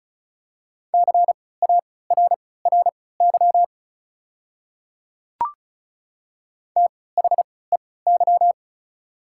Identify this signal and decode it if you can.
Morse code, heard as a keyed tone, about 36 wpm, spaced as for 20 wpm, 700 Hz. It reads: CARRY THEY
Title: 3x Character Spacing - Top 300 Words - 35wpm